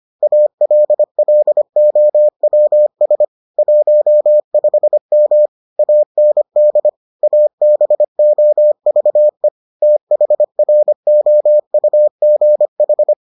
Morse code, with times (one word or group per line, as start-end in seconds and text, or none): 0.2-3.2 ALLOWS
3.6-5.5 15M
5.8-6.9 AND
7.2-9.5 ABOVE
9.8-13.1 THROUGH